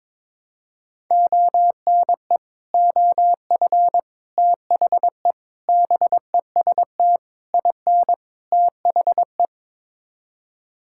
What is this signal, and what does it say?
Morse code, heard as a keyed tone, about 22 wpm, 700 Hz. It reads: ONE OF THE BEST IN THE
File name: practice / 22wpm / Sets of 6 Words - 22wpm